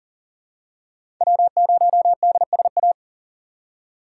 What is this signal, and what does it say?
Morse code, heard as a keyed tone, about 40 wpm, 700 Hz. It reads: W0DSA
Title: USA Call Signs - 5 Characters - 4x Character Spacing - 40wpm